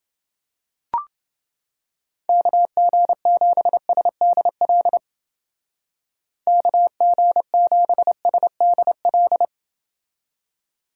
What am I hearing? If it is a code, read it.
Morse code, heard as a keyed tone, about 29 wpm, 700 Hz. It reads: KG7SDL KG7SDL